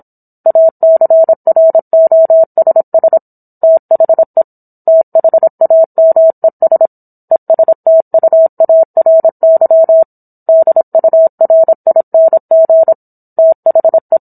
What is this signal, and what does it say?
Morse code, heard as a keyed tone, about 27 wpm, 650 Hz. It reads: ACROSS THE THAMES ESTUARY DURING THE